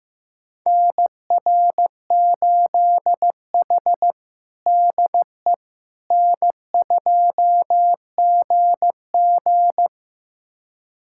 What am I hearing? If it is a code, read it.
Morse code, heard as a keyed tone, about 16 wpm, 700 Hz. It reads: NR8H DE N2GG
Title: USA Callsign DE Callsign - 4 Characters - 15wpm